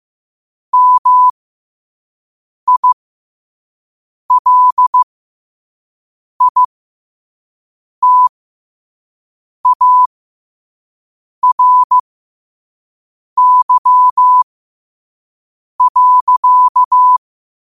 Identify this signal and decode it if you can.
Morse code, heard as a keyed tone, about 15 wpm, spaced as for 5 wpm, 1000 Hz. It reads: MILITARY.
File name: qotdfiles / 5wpm